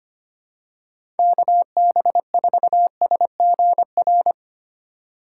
Morse code, heard as a keyed tone, about 25 wpm, 700 Hz. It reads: KB4SGR
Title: USA Call Signs - 6 Characters - 3x Character Spacing - 25wpm